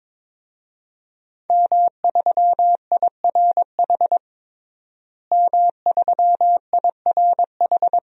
Morse code, heard as a keyed tone, about 22 wpm, 700 Hz. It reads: M3IRH M3IRH
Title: DXCC-100 Call Signs - 5 Characters - Repeated 2x - 22wpm